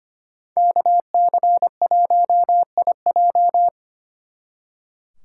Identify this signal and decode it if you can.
Morse code, heard as a keyed tone, about 30 wpm, 700 Hz. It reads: KC1IJ
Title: Speed Racing - USA 2x2 Call Signs 25wpm